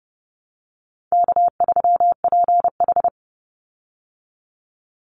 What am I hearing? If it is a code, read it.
Morse code, heard as a keyed tone, about 30 wpm, 700 Hz. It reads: K3PH
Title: Easy Call Signs Repeated 3x - 30wpm